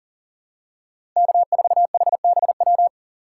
Morse code, heard as a keyed tone, about 37 wpm, 700 Hz. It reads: K4HBW
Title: USA Call Signs - 5 Characters - 6x Character Spacing - 40wpm